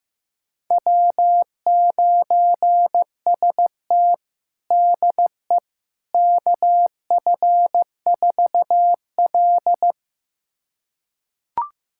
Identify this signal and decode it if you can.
Morse code, heard as a keyed tone, about 15 wpm, 700 Hz. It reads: W9ST DE KF4L
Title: USA Callsign DE Callsign - 4 Characters - 15wpm